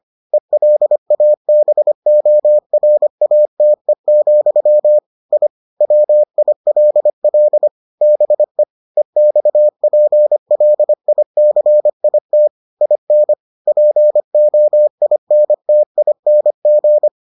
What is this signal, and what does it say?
Morse code, heard as a keyed tone, about 25 wpm, 600 Hz. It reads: ELABORATE, I WILL BE EXPLICIT IN POINTING